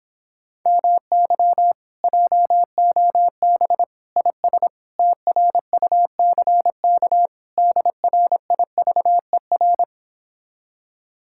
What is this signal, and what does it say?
Morse code, heard as a keyed tone, about 26 wpm, 700 Hz. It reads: MY JOB IS TRUCK DRIVER